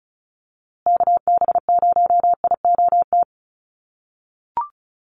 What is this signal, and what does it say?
Morse code, heard as a keyed tone, about 34 wpm, 700 Hz. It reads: KB0IOT
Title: USA Call Signs Repeated 2x 35wpm